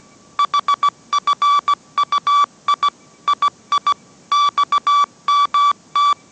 Morse code, heard as a keyed tone, about 17 wpm, 1150 Hz. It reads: HFUI II XMT